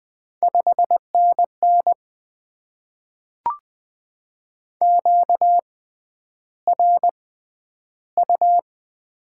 Morse code, heard as a keyed tone, about 20 wpm, spaced as for 10 wpm, 700 Hz. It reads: 5NN QRU
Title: Top 90 QSO Elements - 6x Character Spacing - 20wpm